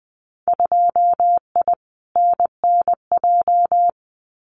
Morse code, heard as a keyed tone, about 20 wpm, 700 Hz. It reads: 2I NNJ
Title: Winter Field Day Exchanges Encore 20wpm